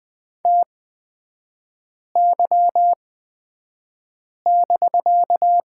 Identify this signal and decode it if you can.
Morse code, heard as a keyed tone, about 20 wpm, spaced as for 5 wpm, 700 Hz. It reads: TY<BK>
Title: 250 Lesson 40e - BK - Three Characters 20wpm